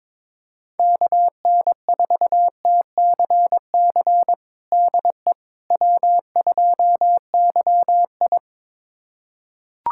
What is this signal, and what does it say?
Morse code, heard as a keyed tone, about 22 wpm, 700 Hz. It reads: KN4TCC DE W2YI
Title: USA Callsign DE Callsign 22wpm